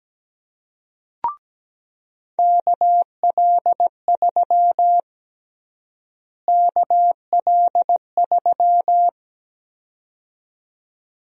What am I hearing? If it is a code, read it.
Morse code, heard as a keyed tone, about 17 wpm, 700 Hz. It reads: KL3 KL3
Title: USA Call Sign Prefixes - Repeated 2x - 17wpm